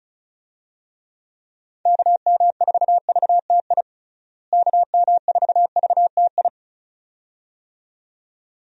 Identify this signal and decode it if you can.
Morse code, heard as a keyed tone, about 36 wpm, 700 Hz. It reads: KM4VTI KM4VTI
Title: USA Call Signs - 6 Characters - Repeated 2x - 35wpm